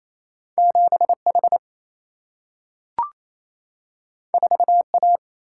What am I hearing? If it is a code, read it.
Morse code, heard as a keyed tone, about 28 wpm, 700 Hz. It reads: 7H 4A
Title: Single-Letter Single-Number 28wpm